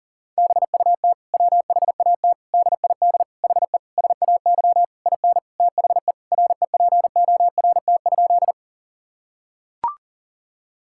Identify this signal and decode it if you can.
Morse code, heard as a keyed tone, about 37 wpm, 700 Hz. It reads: BUT WHAT DID HE SAY IN THE REPORT?
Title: Sentences from Top 1000 Words - Review Part 6 - 40wpm